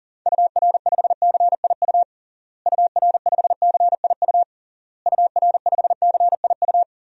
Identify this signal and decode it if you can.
Morse code, heard as a keyed tone, about 40 wpm, 700 Hz. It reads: UR5CIU UR5CIU UR5CIU
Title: DXCC-100 Call Signs - 6 Characters - Repeated 3x - 40wpm